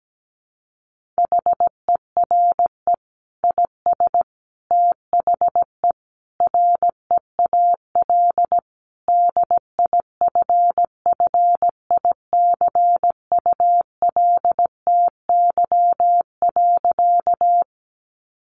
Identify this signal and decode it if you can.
Morse code, heard as a keyed tone, about 17 wpm, 700 Hz. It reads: HERE IS THE REAL DIFFICULTY.
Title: Sentences from Top 1500 Words 17wpm